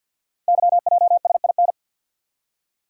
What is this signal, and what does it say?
Morse code, heard as a keyed tone, about 51 wpm, 700 Hz. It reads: YJSIN
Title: Code Groups - 5 Letters - 50wpm